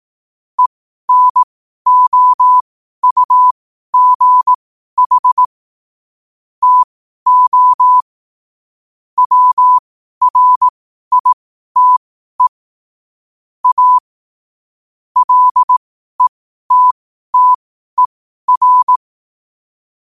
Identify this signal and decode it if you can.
Morse code, heard as a keyed tone, about 18 wpm, spaced as for 12 wpm, 1000 Hz. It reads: ENOUGH TO WRITE A LETTER
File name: qotdfiles / 12wpm